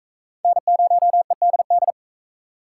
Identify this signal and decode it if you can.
Morse code, heard as a keyed tone, about 36 wpm, 700 Hz. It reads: N0EDD